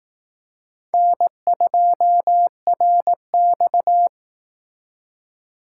Morse code, heard as a keyed tone, about 18 wpm, 700 Hz. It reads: N2RX
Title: Speed Racing - USA 1x2 Call Signs 15wpm